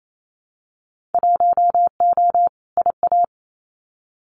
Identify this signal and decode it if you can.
Morse code, heard as a keyed tone, about 28 wpm, 700 Hz. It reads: 1O IA
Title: Winter Field Day Exchanges Encore 28wpm